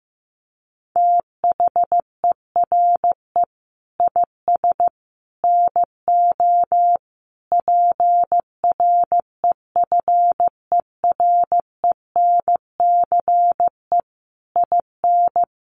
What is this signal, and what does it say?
Morse code, heard as a keyed tone, about 15 wpm, 700 Hz. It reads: THERE IS NO PREFERENCE IN
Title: Sentences from Top 1500 Words 15wpm